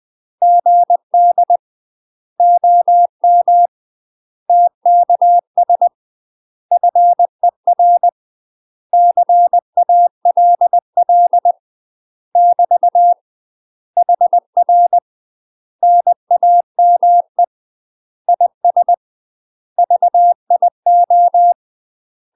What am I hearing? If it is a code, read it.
Morse code, heard as a keyed tone, about 20 wpm, 700 Hz. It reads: GD OM TKS FER CALL <BT> HR NAME IS VIO
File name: UK MASTER QSOs TRAINING / 20WPM / UK MASTER QSOs 0010